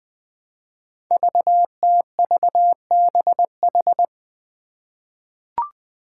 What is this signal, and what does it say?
Morse code, heard as a keyed tone, about 20 wpm, 700 Hz. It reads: VTVBH
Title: Five Challenging Letter-Number Combinations 20wpm